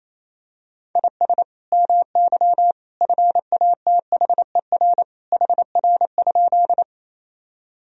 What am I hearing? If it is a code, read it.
Morse code, heard as a keyed tone, about 28 wpm, 700 Hz. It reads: IS MY FATHER HR?
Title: Sentences from Top 300 Words and Abbreviations 28wpm